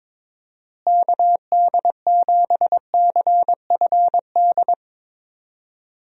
Morse code, heard as a keyed tone, about 22 wpm, 700 Hz. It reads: KD7CFD
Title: USA Call Signs - 6 Characters - Repeated 3x - 22wpm